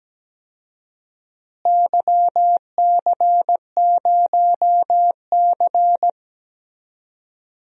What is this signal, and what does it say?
Morse code, heard as a keyed tone, about 17 wpm, 700 Hz. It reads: YC0C
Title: DXCC-100 Call Signs - 4 Characters - Repeated 2x - 17wpm